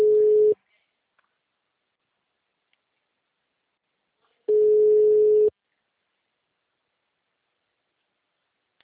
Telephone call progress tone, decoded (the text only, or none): ringback tone